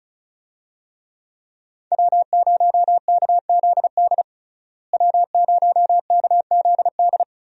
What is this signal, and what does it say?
Morse code, heard as a keyed tone, about 34 wpm, 700 Hz. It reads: W0KZD W0KZD
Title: USA Call Signs - 5 Characters - Repeated 2x - 35wpm